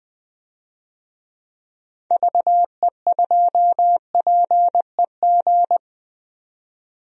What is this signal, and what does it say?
Morse code, heard as a keyed tone, about 20 wpm, 700 Hz. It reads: VE2PEG
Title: DXCC-100 Call Signs - 6 Characters - 20wpm